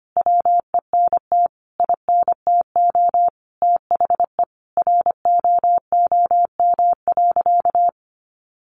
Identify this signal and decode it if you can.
Morse code, heard as a keyed tone, about 25 wpm, 700 Hz. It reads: WENT INTO THE ROOM.
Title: Sentences from Top 600 Words 25wpm